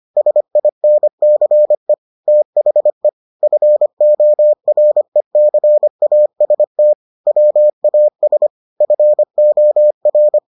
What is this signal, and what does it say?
Morse code, heard as a keyed tone, about 25 wpm, 600 Hz. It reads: SINCE THE FORECAST WAS FOR